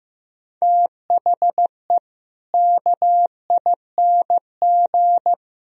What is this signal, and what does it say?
Morse code, heard as a keyed tone, about 15 wpm, 700 Hz. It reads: THE KING
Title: Sentences from Top 1000 Words - Review Part 2 - 15wpm